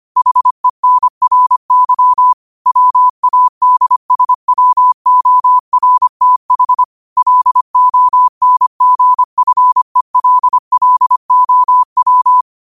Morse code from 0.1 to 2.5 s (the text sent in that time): HENRY